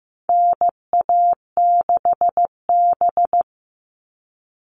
Morse code, heard as a keyed tone, about 15 wpm, 700 Hz. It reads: NA6B
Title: All Cut Numbers - 4-Digit 15wpm